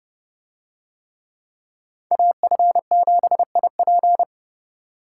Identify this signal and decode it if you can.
Morse code, heard as a keyed tone, about 30 wpm, 700 Hz. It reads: AF7IP